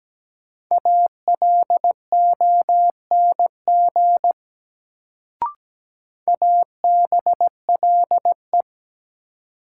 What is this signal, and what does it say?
Morse code, heard as a keyed tone, about 17 wpm, 700 Hz. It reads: ALONG ABLE
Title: Sentences from Top 700 Words - New Word Drill - 17wpm